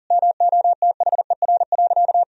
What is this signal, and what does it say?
Morse code, heard as a keyed tone, about 40 wpm, 700 Hz. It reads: MOTHER.
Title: Sentences from Top 1000 Words - Review Part 3 - 40wpm